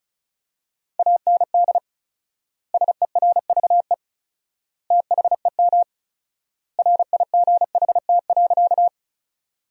Morse code, heard as a keyed tone, about 36 wpm, 700 Hz. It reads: AND SERVE THEM RIGHT.
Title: 4x Word-Spacing - Sentences from Top 400 Words - 35wpm